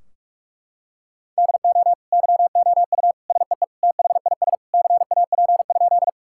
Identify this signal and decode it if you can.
Morse code, heard as a keyed tone, about 46 wpm, 700 Hz. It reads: DO YOU SEE THIS CAP?
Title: Sentences from Top 1600 Words 45wpm